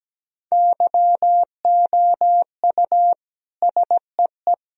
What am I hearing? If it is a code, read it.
Morse code, heard as a keyed tone, about 17 wpm, 700 Hz. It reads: YOU SEE
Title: Sentences from Top 1000 Words - Review Part 1 - 17wpm